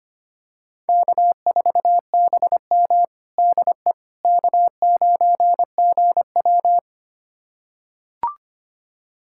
Morse code, heard as a keyed tone, about 25 wpm, 700 Hz. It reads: K4BM DE K9GW
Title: USA Callsign DE Callsign - 4 Characters - 25wpm